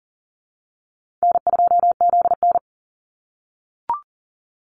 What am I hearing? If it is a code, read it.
Morse code, heard as a keyed tone, about 40 wpm, 700 Hz. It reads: N2ZN
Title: Easy Call Signs - 40wpm